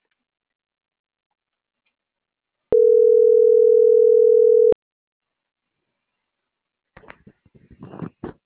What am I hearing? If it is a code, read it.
Ringback tone (North America), 440 + 480 Hz.